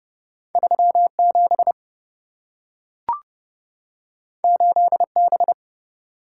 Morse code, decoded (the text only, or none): 37 8B